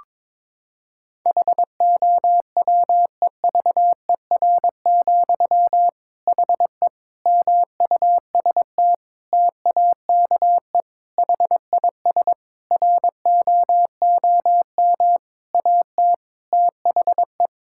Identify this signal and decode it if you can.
Morse code, heard as a keyed tone, about 22 wpm, 700 Hz. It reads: HOWEVER, HE MUST TAKE HIS ROOM AT THE
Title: Sentences from Top 1600 Words 22wpm